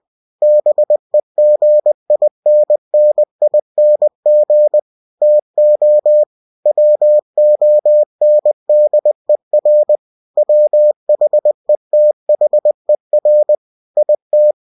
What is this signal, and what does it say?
Morse code, heard as a keyed tone, about 20 wpm, 600 Hz. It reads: BEGINNING TO WONDER WHETHER IT
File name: cw-books / winnie-the-pooh / winnie-20wpm-page-0091